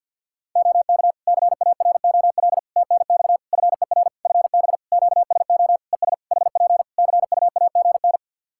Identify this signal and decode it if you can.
Morse code, heard as a keyed tone, about 48 wpm, 700 Hz. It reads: OK CAROL TNX FER FB QSO ES HP CUAGN
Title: Practice QSO - Exchange 4 - Ending - 50wpm